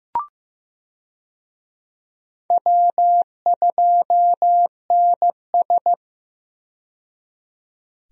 Morse code, heard as a keyed tone, about 15 wpm, 700 Hz. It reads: W2NS
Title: USA Call Signs - 4 Characters - 15wpm